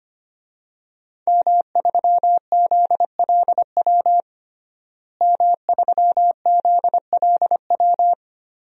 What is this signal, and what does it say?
Morse code, heard as a keyed tone, about 25 wpm, 700 Hz. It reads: M3ZLW M3ZLW
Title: DXCC-100 Call Signs - 5 Characters - Repeated 2x - 25wpm